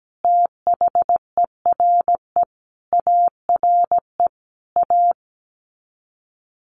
Morse code, heard as a keyed tone, about 17 wpm, 700 Hz. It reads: THERE ARE A
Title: Sets of 5 Words Encore2 17wpm